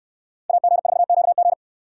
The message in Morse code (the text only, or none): AG4QG